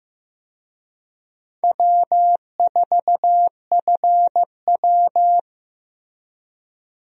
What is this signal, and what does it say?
Morse code, heard as a keyed tone, about 15 wpm, 700 Hz. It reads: W4FW
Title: USA Call Signs - 4 Characters - Repeated 2x - 15wpm